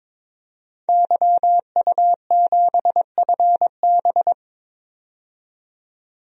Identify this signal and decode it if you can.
Morse code, heard as a keyed tone, about 22 wpm, 700 Hz. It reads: YU7FB
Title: DXCC-100 Call Signs - 5 Characters - 22wpm